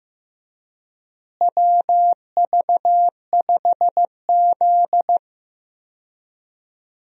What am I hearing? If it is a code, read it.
Morse code, heard as a keyed tone, about 15 wpm, 700 Hz. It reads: WV5Z
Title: USA Call Signs - 4 Characters - Repeated 3x - 15wpm